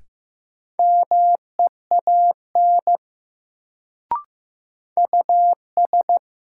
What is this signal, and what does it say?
Morse code, heard as a keyed tone, about 15 wpm, 700 Hz. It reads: MEAN US
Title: Sentences from Top 200 Words - New Word Drill - 15wpm